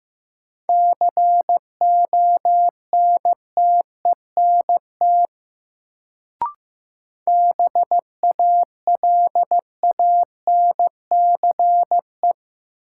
Morse code, heard as a keyed tone, about 15 wpm, 700 Hz. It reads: CONTENT BALANCE
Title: Sentences from Top 900 Words - New Word Drill - 15wpm